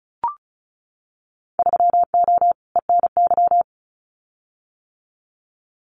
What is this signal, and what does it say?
Morse code, heard as a keyed tone, about 34 wpm, 700 Hz. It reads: 3O ENY
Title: Winter Field Day Exchanges Encore 35wpm